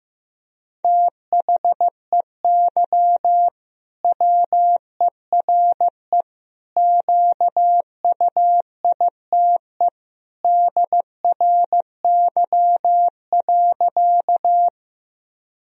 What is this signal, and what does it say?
Morse code, heard as a keyed tone, about 15 wpm, 700 Hz. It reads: THEY WERE QUITE DRY.
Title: Sentences from Top 1000 Words - Review Part 3 - 15wpm